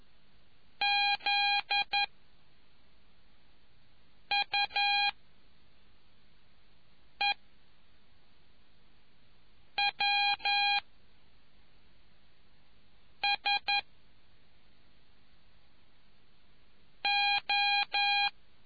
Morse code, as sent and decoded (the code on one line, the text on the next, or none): --.. / ..- / . / .-- / ... / ---
Z U E W S O